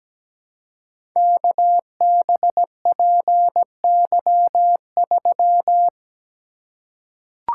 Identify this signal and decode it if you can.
Morse code, heard as a keyed tone, about 17 wpm, 700 Hz. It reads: KBPY3
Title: Code Groups - 5 Letter:Numbers - 17wpm